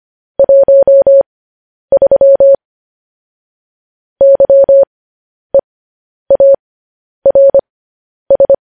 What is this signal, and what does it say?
Morse code, heard as a keyed tone, about 25 wpm, spaced as for 10 wpm, 550 Hz. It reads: 13 YEARS